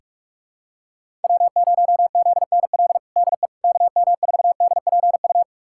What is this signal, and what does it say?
Morse code, heard as a keyed tone, about 46 wpm, 700 Hz. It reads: W0ZNL DE KM4DPU
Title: USA Callsign DE Callsign 45wpm